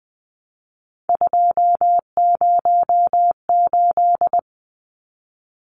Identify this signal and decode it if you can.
Morse code, heard as a keyed tone, about 20 wpm, 700 Hz. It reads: 208
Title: Numbers - 3-Digit 20wpm